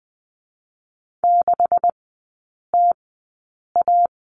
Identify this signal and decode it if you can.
Morse code, heard as a keyed tone, about 20 wpm, spaced as for 8 wpm, 700 Hz. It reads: 6TA
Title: 152 Lesson 25e - Letter p - Three Characters 20wpm